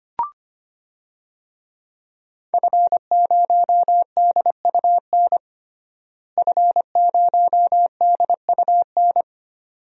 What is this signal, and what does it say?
Morse code, heard as a keyed tone, about 25 wpm, 700 Hz. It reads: F0DUN F0DUN
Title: DXCC-100 Call Signs - 5 Characters - Repeated 2x - 25wpm